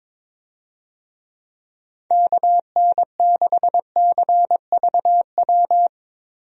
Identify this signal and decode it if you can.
Morse code, heard as a keyed tone, about 22 wpm, 700 Hz. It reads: KN6CVW
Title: USA Call Signs - 6 Characters - 22wpm